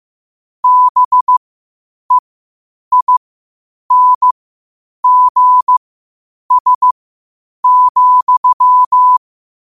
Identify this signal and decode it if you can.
Morse code, heard as a keyed tone, about 15 wpm, spaced as for 8 wpm, 1000 Hz. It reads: BEINGS,
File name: qotdfiles / 8wpm